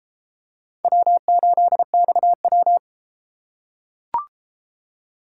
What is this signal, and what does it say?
Morse code, heard as a keyed tone, about 33 wpm, 700 Hz. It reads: W8XW